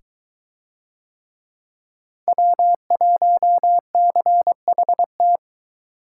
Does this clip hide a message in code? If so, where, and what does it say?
Yes: Morse code from 2.3 to 5.4 s, heard as a keyed tone, about 22 wpm, 700 Hz. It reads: W1CHT